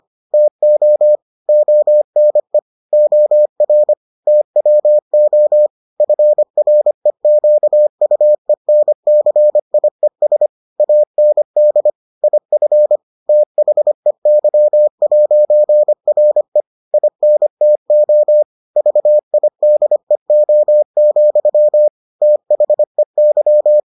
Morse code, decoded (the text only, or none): TO ONE OR TWO FREQUENCIES AND IF THEY'RE INTO VIDEO, THEY